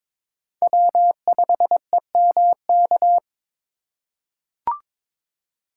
Morse code, heard as a keyed tone, about 22 wpm, 700 Hz. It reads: W5EMK
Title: USA Call Signs - 5 Characters - 4x Character Spacing - 22wpm